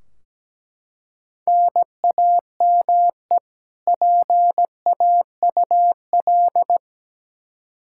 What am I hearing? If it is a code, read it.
Morse code, heard as a keyed tone, about 17 wpm, 700 Hz. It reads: NAME PAUL
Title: Practice QSO - Exchange 1 - Beginning - 17wpm